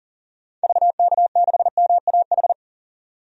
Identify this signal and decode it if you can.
Morse code, heard as a keyed tone, about 40 wpm, 700 Hz. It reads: VK6MAH
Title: DXCC-100 Call Signs - 6 Characters - 40wpm